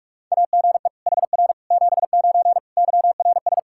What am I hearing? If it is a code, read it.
Morse code, heard as a keyed tone, about 41 wpm, 700 Hz. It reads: AGE HR 79 YRS
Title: K1USN SST QSO Practice 45wpm